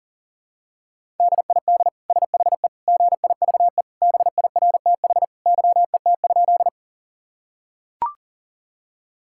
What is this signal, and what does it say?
Morse code, heard as a keyed tone, about 40 wpm, 700 Hz. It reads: DID SHE GIVE BIRTH YET?
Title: Sentences from Top 1000 Words - Review Part 5 - 40wpm